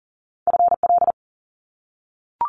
Morse code, heard as a keyed tone, about 42 wpm, 700 Hz. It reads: FL
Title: US States 40wpm